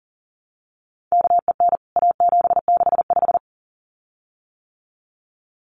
Morse code, heard as a keyed tone, about 40 wpm, 700 Hz. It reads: KEN A765
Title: CWT Exchange 40wpm